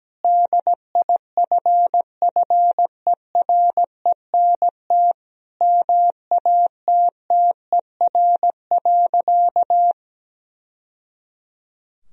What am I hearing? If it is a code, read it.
Morse code, heard as a keyed tone, about 17 wpm, 700 Hz. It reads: DIFFERENT MATTER.
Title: Sentences from Top 1000 Words - Review Part 3 - 17wpm